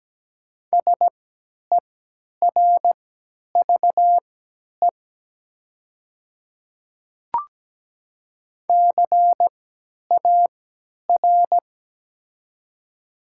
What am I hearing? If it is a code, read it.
Morse code, heard as a keyed tone, about 17 wpm, spaced as for 10 wpm, 700 Hz. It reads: SERVE CAR
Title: 3x Character Spacing - Top 500 Words - 17wpm